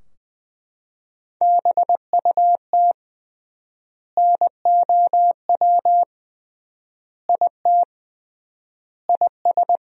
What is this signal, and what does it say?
Morse code, heard as a keyed tone, about 20 wpm, 700 Hz. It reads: BUT NOW IT IS
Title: 3x Word-Spacing - Sentences from Top 300 Words - 20wpm